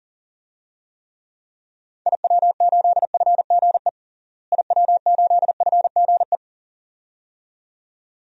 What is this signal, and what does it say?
Morse code, heard as a keyed tone, about 40 wpm, 700 Hz. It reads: IW8FGE IW8FGE
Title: DXCC-100 Call Signs - 6 Characters - Repeated 2x - 40wpm